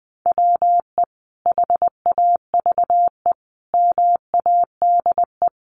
Morse code, heard as a keyed tone, about 20 wpm, 700 Hz. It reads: WE HAVE MADE
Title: Sentences from Top 1400 Words 20wpm